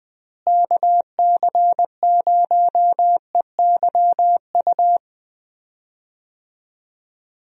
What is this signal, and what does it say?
Morse code, heard as a keyed tone, about 20 wpm, 700 Hz. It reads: KC0EYU
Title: USA Call Signs - 6 Characters - Repeated 3x - 20wpm